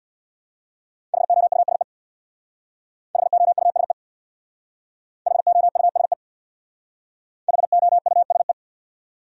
Morse code, heard as a keyed tone, about 60 wpm, 700 Hz. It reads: HOUSE HOUSE HOUSE HOUSE
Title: Speed-Racing Top 500 Words 50wpm